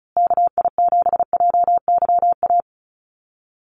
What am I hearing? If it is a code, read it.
Morse code, heard as a keyed tone, about 34 wpm, 700 Hz. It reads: KI7JYA
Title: USA Call Signs Repeated 3x 35wpm